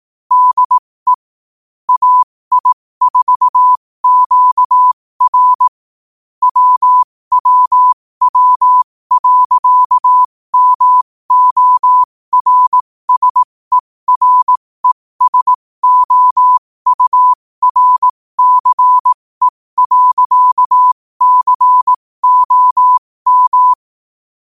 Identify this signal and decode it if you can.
Morse code, heard as a keyed tone, about 18 wpm, spaced as for 15 wpm, 1000 Hz. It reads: DE AI4QR WWW.MORSERESOURCE.COM
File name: qotdfiles / 15wpm